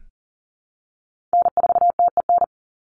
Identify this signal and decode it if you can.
Morse code, heard as a keyed tone, about 40 wpm, 700 Hz. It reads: N4TEN
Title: USA Call Signs 40wpm